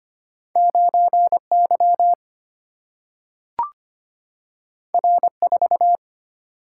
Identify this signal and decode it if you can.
Morse code, heard as a keyed tone, about 25 wpm, 700 Hz. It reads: 9 Y R 4
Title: Single-Letter Single-Number 25wpm